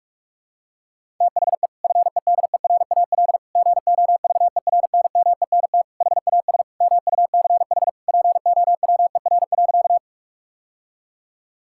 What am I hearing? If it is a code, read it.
Morse code, heard as a keyed tone, about 42 wpm, 700 Hz. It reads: THE FEDERAL GOVERNMENT HAS MUCH POWER.
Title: Sentences from Top 1000 Words - Review Part 5 - 45wpm